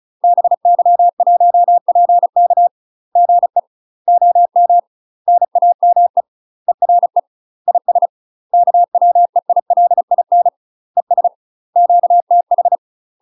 Morse code, heard as a keyed tone, about 34 wpm, 700 Hz. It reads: BY1PK GE OM NAME ERE IS KWEILIN ES QTH
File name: UK MASTER QSOs TRAINING/35WPM